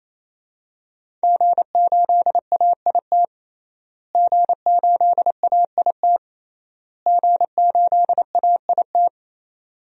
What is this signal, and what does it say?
Morse code, heard as a keyed tone, about 27 wpm, 700 Hz. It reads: G8AIT G8AIT G8AIT
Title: DXCC-100 Call Signs - 5 Characters - Repeated 3x - 28wpm